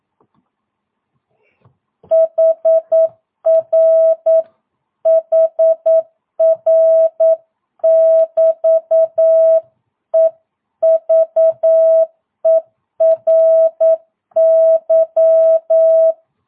Morse code, heard as a keyed tone, about 9 wpm, 650 Hz. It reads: HR HR <BT> EVERY
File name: jh8cbh / 2022cw